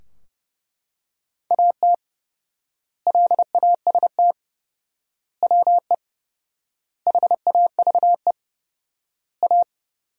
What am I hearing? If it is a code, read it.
Morse code, heard as a keyed tone, about 30 wpm, 700 Hz. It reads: AT LAST WE HAVE A
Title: 4x Word-Spacing - Sentences from Top 300 Words - 30wpm